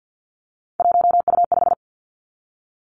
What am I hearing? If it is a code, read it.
Morse code, heard as a keyed tone, about 50 wpm, 700 Hz. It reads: 1U5
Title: Three Challenging Letter-Number Combinations 50wpm